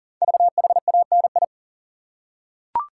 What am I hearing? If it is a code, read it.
Morse code, heard as a keyed tone, about 40 wpm, 700 Hz. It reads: VHANI
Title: Code Groups - 5 Letters - 40wpm